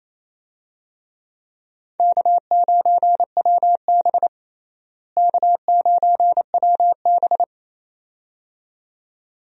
Morse code, heard as a keyed tone, about 28 wpm, 700 Hz. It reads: K9WB K9WB
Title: USA Call Signs - 4 Characters - Repeated 2x - 28wpm